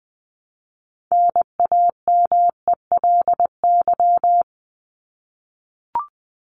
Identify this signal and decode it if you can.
Morse code, heard as a keyed tone, about 20 wpm, 700 Hz. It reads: NAMELY
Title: LY Suffix - Top 100 Words with Base Words 20wpm